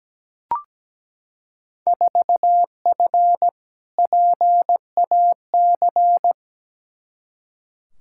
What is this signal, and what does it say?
Morse code, heard as a keyed tone, about 17 wpm, 700 Hz. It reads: 4F PAC
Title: Field Day Exchanges Encore - 17wpm